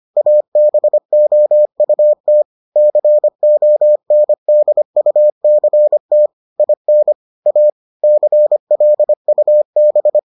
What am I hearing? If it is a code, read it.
Morse code, heard as a keyed tone, about 25 wpm, 600 Hz. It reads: ABOUT CONDUCT IN A CLUB